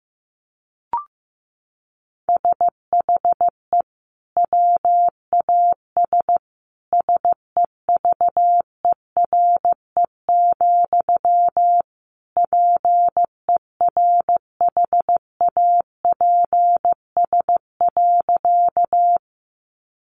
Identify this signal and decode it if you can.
Morse code, heard as a keyed tone, about 15 wpm, 700 Hz. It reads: SHE WAS SEVERE, PERHAPS.
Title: Sentences from Top 1400 Words 15wpm